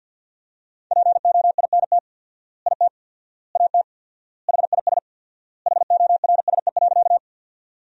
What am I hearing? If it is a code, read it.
Morse code, heard as a keyed tone, about 50 wpm, 700 Hz. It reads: POINT IT AT HIS HORSE.
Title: 4x Word-Spacing - Sentences from Top 300 Words - 50wpm